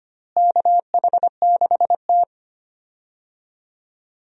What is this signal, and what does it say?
Morse code, heard as a keyed tone, about 25 wpm, 700 Hz. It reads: KH6T
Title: DXCC-100 Call Signs - 4 Characters - Repeated 2x - 25wpm